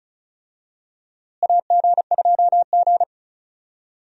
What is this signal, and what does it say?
Morse code, heard as a keyed tone, about 36 wpm, 700 Hz. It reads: AG2G
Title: USA Call Signs - 4 Characters - Repeated 3x - 35wpm